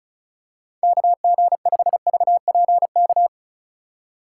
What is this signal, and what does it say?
Morse code, heard as a keyed tone, about 34 wpm, 700 Hz. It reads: KG5VPK
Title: USA Call Signs - 4x Character Spacing - 35wpm